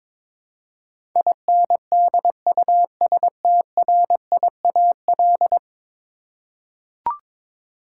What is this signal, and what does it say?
Morse code, heard as a keyed tone, about 21 wpm, 700 Hz. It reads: INDUSTRIAL
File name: practice / 22wpm / 10 Letter Words - First 3 Letters Given - 22wpm